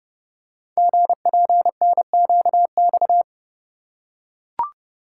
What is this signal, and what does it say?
Morse code, heard as a keyed tone, about 29 wpm, 700 Hz. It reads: GPNQX
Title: Code Groups - 5 Letter:Numbers - 30wpm